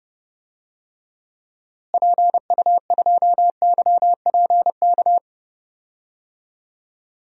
Morse code, heard as a keyed tone, about 32 wpm, 700 Hz. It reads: PU2YPK
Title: DXCC-100 Call Signs - 6 Characters - Repeated 2x - 30wpm